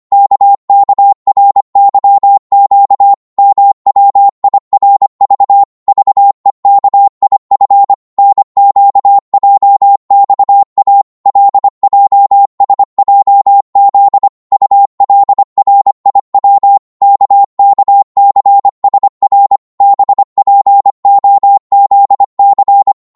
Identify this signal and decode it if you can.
Morse code, heard as a keyed tone, about 25 wpm, 800 Hz. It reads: KKRYQ MWIRV VEKIF NQJXA LJSJZ ULRIW KKCSR BPOZC
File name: Morse / ltr-25wpm